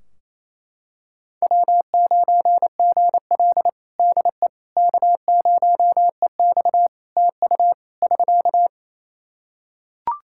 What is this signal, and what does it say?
Morse code, heard as a keyed tone, about 28 wpm, 700 Hz. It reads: W9GL DE K0EX TU <SK>
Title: Practice QSO - Exchange 4 - Ending - 28wpm